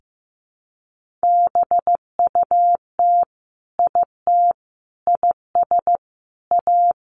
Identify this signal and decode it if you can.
Morse code, heard as a keyed tone, about 15 wpm, 700 Hz. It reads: BUT IT IS A
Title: Sentences from Top 1500 Words 15wpm